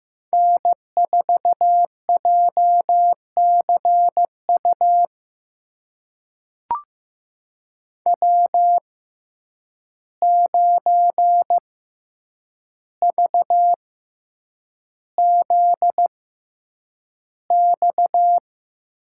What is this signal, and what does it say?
Morse code, heard as a keyed tone, about 15 wpm, spaced as for 8 wpm, 700 Hz. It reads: N4JCU W9VZX
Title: USA Call Signs - 5 Characters - 6x Character Spacing - 15wpm